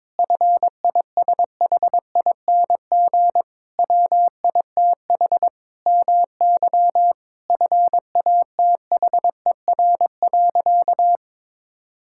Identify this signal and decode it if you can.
Morse code, heard as a keyed tone, about 22 wpm, 700 Hz. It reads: FISHING WITH MY FATHER.
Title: Sentences from Top 1000 Words - Review Part 5 - 22wpm